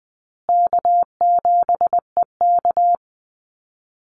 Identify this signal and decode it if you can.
Morse code, heard as a keyed tone, about 20 wpm, 700 Hz. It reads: K7EK
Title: Easy Call Signs Repeated 3x - 20wpm